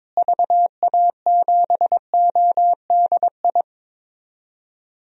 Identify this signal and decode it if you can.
Morse code, heard as a keyed tone, about 22 wpm, 700 Hz. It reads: VA7ODI
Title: DXCC-100 Call Signs - 6 Characters - Repeated 2x - 22wpm